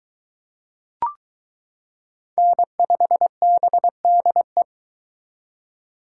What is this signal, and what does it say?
Morse code, heard as a keyed tone, about 22 wpm, 700 Hz. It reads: N5BDE